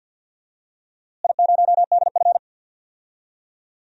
Morse code, heard as a keyed tone, about 53 wpm, 700 Hz. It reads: I0DF
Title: DXCC-100 Call Signs - 4 Characters - Repeated 3x - 50wpm